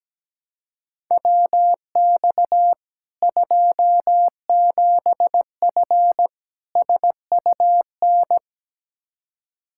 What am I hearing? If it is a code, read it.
Morse code, heard as a keyed tone, about 17 wpm, 700 Hz. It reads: WX 27F SUN
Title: Practice QSO - Exchange 2 - Info - 17wpm